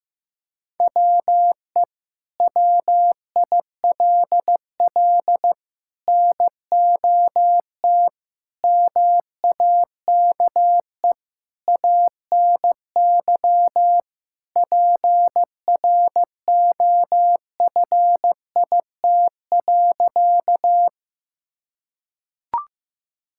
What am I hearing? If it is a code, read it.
Morse code, heard as a keyed tone, about 16 wpm, 700 Hz. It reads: WE WILL NOT MAKE ANY PROFIT.
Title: Sentences from Top 1000 Words - Review Part 5 - 15wpm